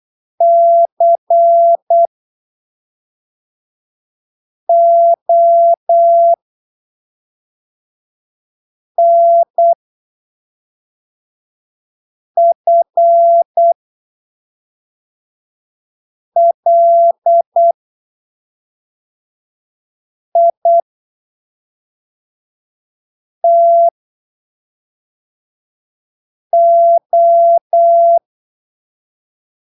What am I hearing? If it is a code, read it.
Morse code, heard as a keyed tone, about 8 wpm, spaced as for 3 wpm, 680 Hz. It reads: CONFLITO
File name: arquivos / texto / texto 30 08ppm